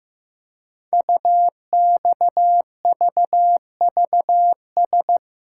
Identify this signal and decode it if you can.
Morse code, heard as a keyed tone, about 15 wpm, 700 Hz. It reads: UXVVS
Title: Code Groups - 5 Letters - 15wpm